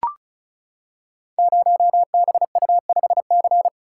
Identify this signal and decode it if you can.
Morse code, heard as a keyed tone, about 36 wpm, 700 Hz. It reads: 0BU5C
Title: Code Groups - 5 Letter:Numbers - 35wpm